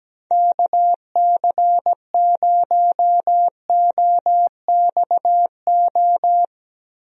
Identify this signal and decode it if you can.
Morse code, heard as a keyed tone, about 17 wpm, 700 Hz. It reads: KC0OXO
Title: USA Call Signs - 6 Characters - Repeated 2x - 17wpm